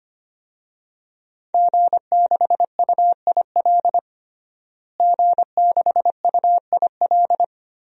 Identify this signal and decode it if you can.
Morse code, heard as a keyed tone, about 25 wpm, 700 Hz. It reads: G6UIL G6UIL
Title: DXCC-100 Call Signs - 5 Characters - Repeated 3x - 25wpm